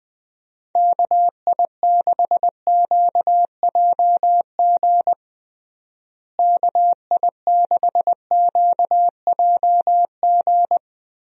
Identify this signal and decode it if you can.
Morse code, heard as a keyed tone, about 20 wpm, 700 Hz. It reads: KI6QJG KI6QJG
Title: USA Call Signs - 6 Characters - Repeated 2x - 20wpm